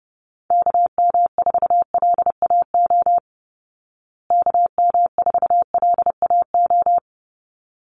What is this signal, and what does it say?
Morse code, heard as a keyed tone, about 30 wpm, 700 Hz. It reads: KM4LAO KM4LAO
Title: Call Signs Repeated 3x - 30wpm